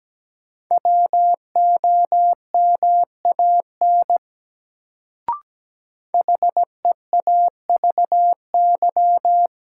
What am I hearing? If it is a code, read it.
Morse code, heard as a keyed tone, about 17 wpm, 700 Hz. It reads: WOMAN HEAVY
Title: Sentences from Top 1100 Words - New Word Drill Alt - 17wpm